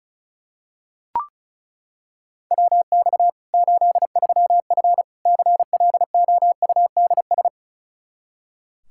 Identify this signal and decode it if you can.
Morse code, heard as a keyed tone, about 36 wpm, 700 Hz. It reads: WX 83F CLOUDS
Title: Practice QSO - Exchange 2 - Info - 35wpm